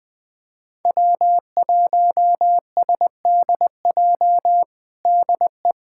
Morse code, heard as a keyed tone, about 20 wpm, 700 Hz. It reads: W1SDJ DE
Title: USA Callsign DE Callsign - 5 Characters - 20wpm